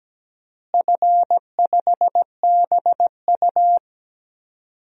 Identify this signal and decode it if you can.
Morse code, heard as a keyed tone, about 17 wpm, 700 Hz. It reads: F5BU
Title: DXCC-100 Call Signs - 4 Characters - Repeated 2x - 17wpm